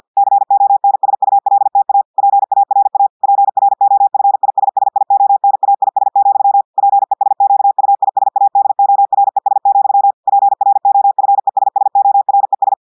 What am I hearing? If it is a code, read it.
Morse code, heard as a keyed tone, about 50 wpm, 800 Hz. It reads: CONSULTA PARA PROFISSIONAIS, PESQUISADORES, PROFESSORES
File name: audios / portugais / portugais texte en clair 1 vitesse 50 mots minutes